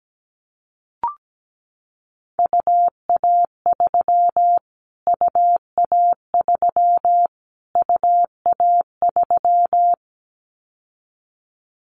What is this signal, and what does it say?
Morse code, heard as a keyed tone, about 17 wpm, 700 Hz. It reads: UA3 UA3 UA3
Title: Callsign Prefixes for Top 25 DXCC - Repeated 3x - 17wpm